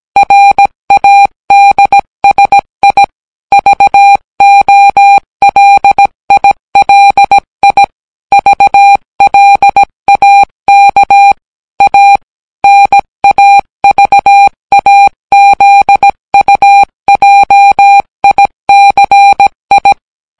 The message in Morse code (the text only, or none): RADSI VOLILI VLAK A NAVAZUJICI